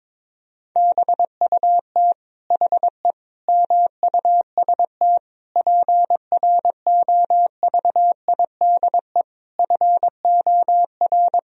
Morse code, heard as a keyed tone, about 22 wpm, 700 Hz. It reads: BUT HE MUST PROVIDE FOR
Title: Sentences from Top 1000 Words - Review Part 3 - 22wpm